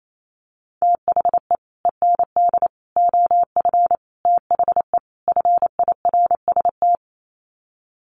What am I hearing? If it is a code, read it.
Morse code, heard as a keyed tone, about 28 wpm, 700 Hz. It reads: THE END OF THE FIRST
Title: Sets of 5 Words Encore 28wpm